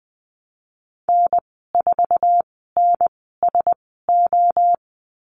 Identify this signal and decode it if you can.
Morse code, heard as a keyed tone, about 20 wpm, spaced as for 14 wpm, 700 Hz. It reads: N4NSO
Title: 062 Lesson 11g - Letter h - Call Signs 20wpm